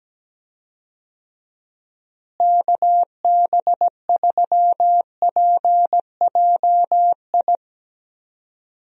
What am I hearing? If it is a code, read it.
Morse code, heard as a keyed tone, about 17 wpm, 700 Hz. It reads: KB3PJI
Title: USA Call Signs - 6 Characters - Repeated 2x - 17wpm